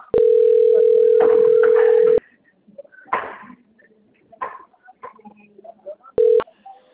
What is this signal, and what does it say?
Ringback tone (North America), 440 + 480 Hz.